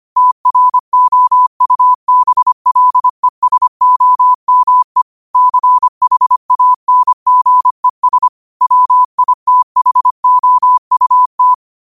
Morse code, heard as a keyed tone, about 25 wpm, 1000 Hz. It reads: TROUBLESOME CHANGES WITHOUT
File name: qotdfiles / 25wpm